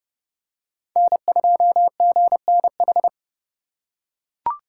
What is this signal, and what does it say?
Morse code, heard as a keyed tone, about 30 wpm, 700 Hz. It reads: N2GNH